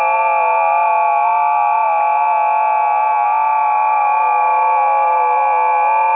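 FT8.
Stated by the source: FT8 test recording of a busy 20 m band, from ft8_lib by Karlis Goba.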